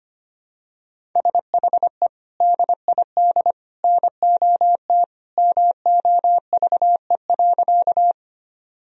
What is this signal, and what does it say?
Morse code, heard as a keyed tone, about 25 wpm, 700 Hz. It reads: SHE DID NOT MOVE.
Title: Sentences from Top 1000 Words - Review Part 1 - 25wpm